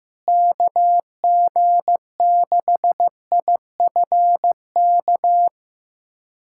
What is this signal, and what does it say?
Morse code, heard as a keyed tone, about 15 wpm, 700 Hz. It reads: KG6IFK
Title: USA Callsign DE Callsign 15wpm